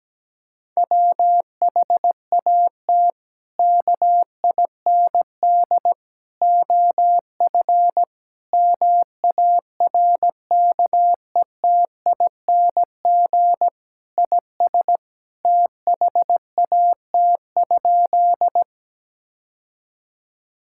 Morse code, heard as a keyed tone, about 17 wpm, 700 Hz. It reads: WHAT KIND OF MARKETING IS THAT?
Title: Sentences from Top 1000 Words - Review Part 5 - 17wpm